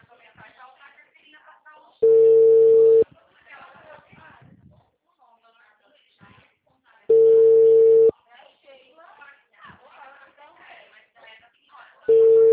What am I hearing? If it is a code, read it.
Ringback tone (China, 450 Hz), with speech.